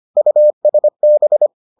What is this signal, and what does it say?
Morse code, heard as a keyed tone, about 25 wpm, 600 Hz. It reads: USB